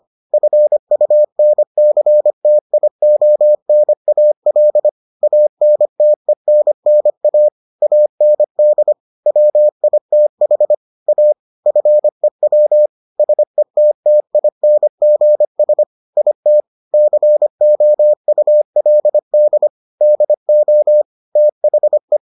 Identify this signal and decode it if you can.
Morse code, heard as a keyed tone, about 25 wpm, 600 Hz. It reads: FUNCTIONAL ANTENNA AND WITH A FEW SETTINGS IT COULD DO THE